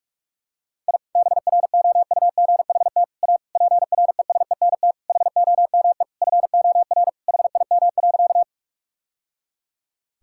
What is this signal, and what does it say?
Morse code, heard as a keyed tone, about 46 wpm, 700 Hz. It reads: I BROUGHT A PRESENT HOME FOR HIM.
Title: Sentences from Top 1000 Words - Review Part 5 - 45wpm